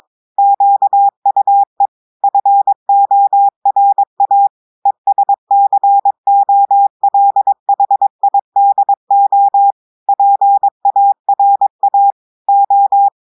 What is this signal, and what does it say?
Morse code, heard as a keyed tone, about 22 wpm, 800 Hz. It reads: QUE FORA ESCOLHIDO PARA O
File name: audios / portugais / portugais texte en clair 1 vitesse 22 mots minutes